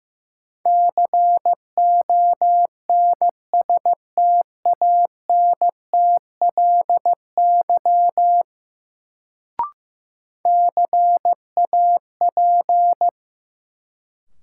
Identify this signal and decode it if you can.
Morse code, heard as a keyed tone, about 15 wpm, 700 Hz. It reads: CONSTANTLY CAP
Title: Sentences from Top 1600 Words - New Word Drill Alt - 15wpm